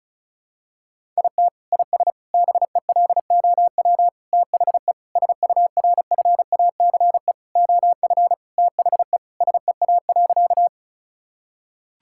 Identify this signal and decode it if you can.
Morse code, heard as a keyed tone, about 34 wpm, 700 Hz. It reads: IT IS BELOW THE SURFACE OF THE SEA.
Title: Sentences from Top 1000 Words - Review Part 5 - 35wpm